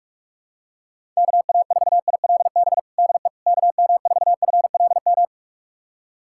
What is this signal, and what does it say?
Morse code, heard as a keyed tone, about 46 wpm, 700 Hz. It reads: KA4ILB DE KM4FLM
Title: USA Callsign DE Callsign 45wpm